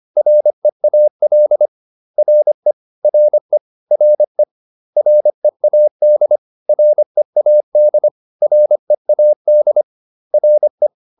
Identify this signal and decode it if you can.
Morse code, heard as a keyed tone, about 25 wpm, 600 Hz. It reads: REAL RE RE RE READ READ READ RE